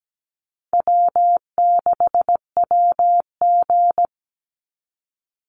Morse code, heard as a keyed tone, about 17 wpm, 700 Hz. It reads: W6WG
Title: Easy Call Signs Repeated 3x - 17wpm